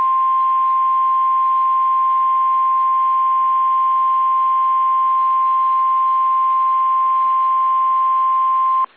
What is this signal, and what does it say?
Emergency Alert System (SAME) tones (the 1050 Hz attention tone). It is heard from 1.2 to 8.7 s.
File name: eas / KSDZKDJL / KSDZKDJL